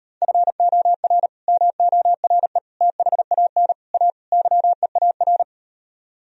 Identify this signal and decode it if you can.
Morse code, heard as a keyed tone, about 39 wpm, 700 Hz. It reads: FOR MORE THAN A YEAR